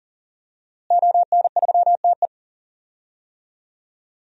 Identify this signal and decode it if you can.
Morse code, heard as a keyed tone, about 40 wpm, 700 Hz. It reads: ON3TE
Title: DXCC-100 Call Signs - 5 Characters - Repeated 2x - 40wpm